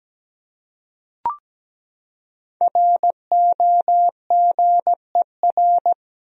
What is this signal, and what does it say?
Morse code, heard as a keyed tone, about 17 wpm, 700 Hz. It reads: ROGER
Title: USA Common Names - 17wpm